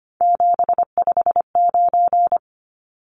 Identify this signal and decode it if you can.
Morse code, heard as a keyed tone, about 25 wpm, 700 Hz. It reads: 759
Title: Numbers - 3-Digit 25wpm